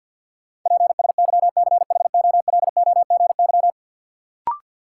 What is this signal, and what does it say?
Morse code, heard as a keyed tone, about 53 wpm, 700 Hz. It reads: PSYCHOLOGY
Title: Sentences from Top 1400 Words - New Word Drill - 50wpm